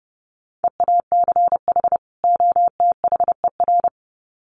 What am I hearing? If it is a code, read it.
Morse code, heard as a keyed tone, about 29 wpm, 700 Hz. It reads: EACH OTHER
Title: Sets of 2 Words 30wpm